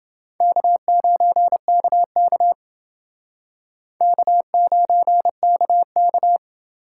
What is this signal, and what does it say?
Morse code, heard as a keyed tone, about 29 wpm, 700 Hz. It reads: K9KK K9KK